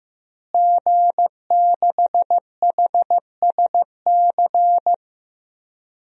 Morse code, heard as a keyed tone, about 15 wpm, 700 Hz. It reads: G6HSC
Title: DXCC-100 Call Signs - 5 Characters - Repeated 2x - 15wpm